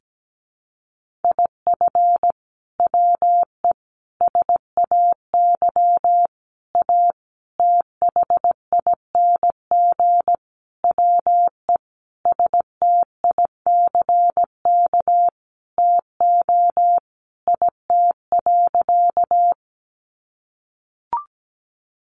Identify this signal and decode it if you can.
Morse code, heard as a keyed tone, about 17 wpm, 700 Hz. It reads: IF WE SAY A THING WE STICK TO IT.
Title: Sentences from Top 1200 Words 17wpm